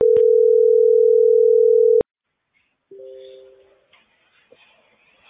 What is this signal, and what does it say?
Ringback tone (North America), 440 + 480 Hz.